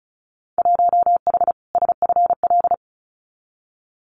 Morse code, heard as a keyed tone, about 34 wpm, 700 Hz. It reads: 1H SFL
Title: Winter Field Day Exchanges 35wpm